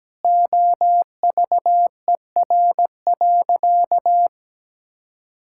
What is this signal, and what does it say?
Morse code, heard as a keyed tone, about 17 wpm, 700 Hz. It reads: OVER.
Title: Sentences from Top 1600 Words 17wpm